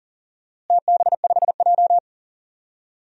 Morse code, heard as a keyed tone, about 40 wpm, 700 Hz. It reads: TB5J
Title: DXCC-100 Call Signs - 4 Characters - Repeated 2x - 40wpm